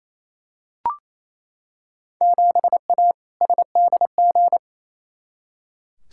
Morse code, heard as a keyed tone, about 28 wpm, 700 Hz. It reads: 7A SDG
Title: Field Day Exchanges Encore - 28wpm